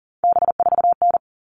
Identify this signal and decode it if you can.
Morse code, heard as a keyed tone, about 40 wpm, 700 Hz. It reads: B4N
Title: All Cut Numbers - 3-Digit 40wpm